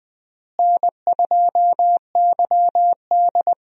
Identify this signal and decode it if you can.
Morse code, heard as a keyed tone, about 20 wpm, 700 Hz. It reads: N2YD
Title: USA Call Signs - 4 Characters - Repeated 3x - 20wpm